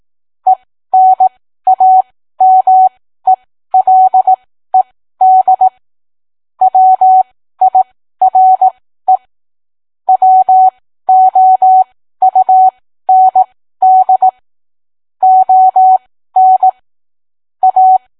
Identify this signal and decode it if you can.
Morse code, heard as a keyed tone, about 18 wpm, spaced as for 13 wpm, 750 Hz. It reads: ENAMELED WIRE WOUND ON A